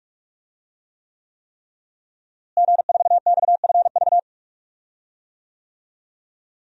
Morse code, heard as a keyed tone, about 46 wpm, 700 Hz. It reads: G4XFV
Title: DXCC-100 Call Signs - 5 Characters - Repeated 3x - 45wpm